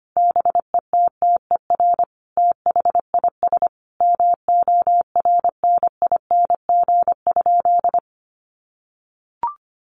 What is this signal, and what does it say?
Morse code, heard as a keyed tone, about 25 wpm, 700 Hz. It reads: BETTER THIS MORNING?
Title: Sentences from Top 1200 Words 25wpm